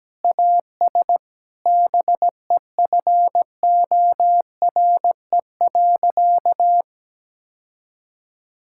Morse code, heard as a keyed tone, about 17 wpm, 700 Hz. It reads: AS BEFORE.
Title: Sentences from Top 1000 Words - Review Part 6 - 17wpm